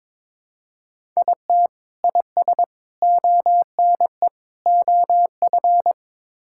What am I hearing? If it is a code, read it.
Morse code, heard as a keyed tone, about 22 wpm, 700 Hz. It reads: IT IS ONE OF
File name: practice / 22wpm / Build A Phrase - 5-Word Phrases - 22wpm